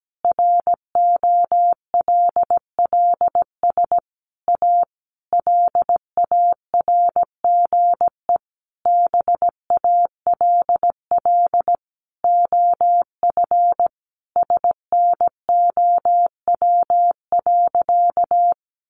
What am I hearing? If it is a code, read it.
Morse code, heard as a keyed tone, about 17 wpm, 700 Hz. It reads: ROLLS A LARGE BALL OF SNOW.
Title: Easy Phrases 17wpm